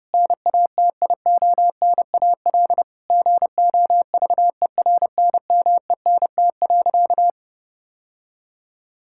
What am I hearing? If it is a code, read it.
Morse code, heard as a keyed tone, about 30 wpm, 700 Hz. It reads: NATIONAL GOVERNMENT.
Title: Sentences from Top 1000 Words - Review Part 6 - 30wpm